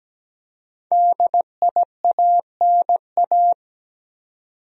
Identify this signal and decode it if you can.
Morse code, heard as a keyed tone, about 17 wpm, 700 Hz. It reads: DIANA
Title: USA Common Names - 17wpm